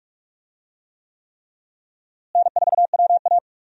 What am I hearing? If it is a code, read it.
Morse code, heard as a keyed tone, about 42 wpm, 700 Hz. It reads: N4WA
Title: USA Call Signs - 4 Characters - Repeated 3x - 45wpm